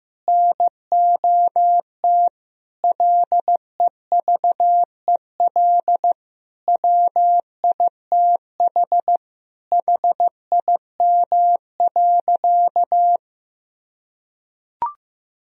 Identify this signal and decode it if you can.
Morse code, heard as a keyed tone, about 15 wpm, 700 Hz. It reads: NOT LEVEL WITH HIM.
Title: Sentences from Top 1000 Words - Review Part 4 - 15wpm